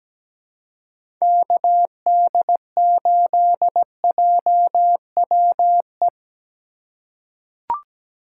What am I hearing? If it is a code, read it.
Morse code, heard as a keyed tone, about 17 wpm, 700 Hz. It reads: KD8JWE
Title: USA Call Signs - 3x Character Spacing - 17wpm